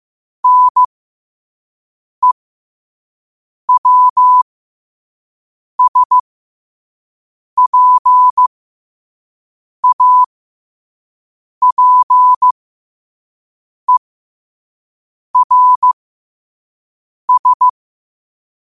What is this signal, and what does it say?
Morse code, heard as a keyed tone, about 15 wpm, spaced as for 5 wpm, 1000 Hz. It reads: NEWSPAPERS
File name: qotdfiles / 5wpm